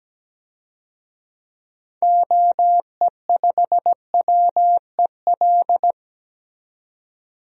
Morse code, heard as a keyed tone, about 17 wpm, 700 Hz. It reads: OE5WEL
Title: DXCC-100 Call Signs - 6 Characters - 17wpm